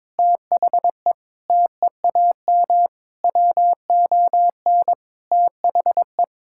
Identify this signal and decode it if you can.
Morse code, heard as a keyed tone, about 22 wpm, 700 Hz. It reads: THE TEAM WON THE
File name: practice / 22wpm / Sentences with 8-Word Phrases - 22wpm